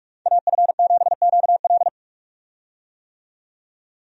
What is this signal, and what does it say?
Morse code, heard as a keyed tone, about 47 wpm, 700 Hz. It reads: AF7QL
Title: USA Call Signs - 5 Characters - 45wpm